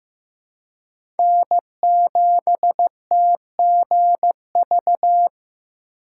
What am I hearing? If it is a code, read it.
Morse code, heard as a keyed tone, about 15 wpm, 700 Hz. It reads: N7TGV
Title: USA Call Signs - 5 Characters - 4x Character Spacing - 15wpm